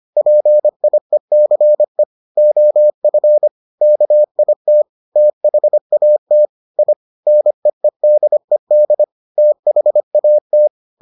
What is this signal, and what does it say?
Morse code, heard as a keyed tone, about 25 wpm, 600 Hz. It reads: PIECE OF KIT THAT I NEEDED THAT